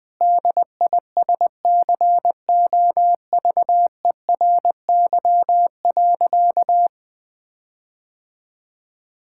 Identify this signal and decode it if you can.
Morse code, heard as a keyed tone, about 20 wpm, 700 Hz. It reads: DISCOVERY.
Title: Sentences with 8-Word Phrases - 20wpm